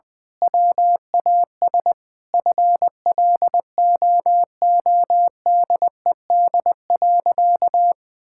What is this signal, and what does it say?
Morse code, heard as a keyed tone, about 21 wpm, 700 Hz. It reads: WAS FLOODED.
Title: Sentences with 6-Word Phrases - 20wpm